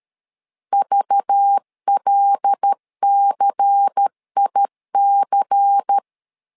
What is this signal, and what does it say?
Morse code, heard as a keyed tone, about 13 wpm, 800 Hz. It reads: VLCIC
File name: LEZIONI / 5 LBVC / 5 LBVC 12wpm 1955